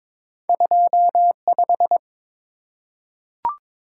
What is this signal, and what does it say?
Morse code, heard as a keyed tone, about 22 wpm, 700 Hz. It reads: 25
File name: practice / 22wpm / Two Letter-Number Combinations - 22wpm